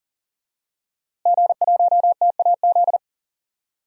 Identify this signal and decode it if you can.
Morse code, heard as a keyed tone, about 40 wpm, 700 Hz. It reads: G1TAZ
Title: DXCC-100 Call Signs - 5 Characters - Repeated 2x - 40wpm